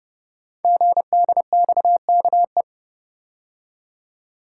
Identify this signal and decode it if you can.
Morse code, heard as a keyed tone, about 29 wpm, 700 Hz. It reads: GDXKE
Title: Code Groups - 5 Letters - 30wpm